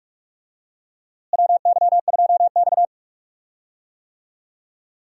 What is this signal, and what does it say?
Morse code, heard as a keyed tone, about 46 wpm, 700 Hz. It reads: WY2X
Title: USA Call Signs - 4 Characters - Repeated 2x - 45wpm